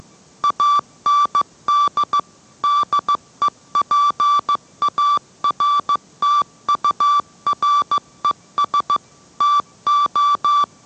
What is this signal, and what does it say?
Morse code, heard as a keyed tone, about 15 wpm, 1160 Hz. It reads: AND DEPARTURES TO